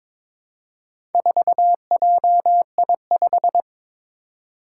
Morse code, heard as a keyed tone, about 22 wpm, 700 Hz. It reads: 4JI5
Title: Four Challenging Letter-Number Combinations 22wpm